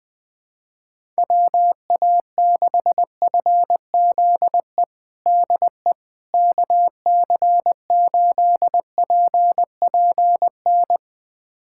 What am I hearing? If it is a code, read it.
Morse code, heard as a keyed tone, about 21 wpm, 700 Hz. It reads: WA6FZE DE KC8PPN
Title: USA Callsign DE Callsign - 6 Characters - 20wpm